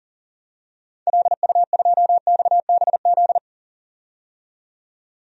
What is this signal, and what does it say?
Morse code, heard as a keyed tone, about 39 wpm, 700 Hz. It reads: LU2XBZ
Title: DXCC-100 Call Signs - 6 Characters - 40wpm